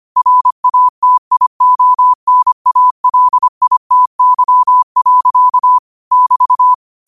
Morse code, heard as a keyed tone, about 25 wpm, 1000 Hz. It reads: RATIONALITY. <BT>
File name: qotdfiles / 25wpm